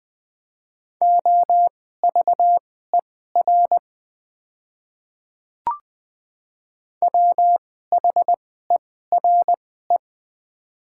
Morse code, heard as a keyed tone, about 20 wpm, spaced as for 14 wpm, 700 Hz. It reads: OVER WHERE
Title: 2x Character Spacing - Top 200 Words - 20wpm